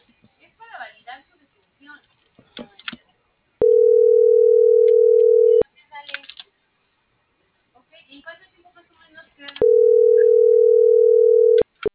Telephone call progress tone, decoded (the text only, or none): ringback tone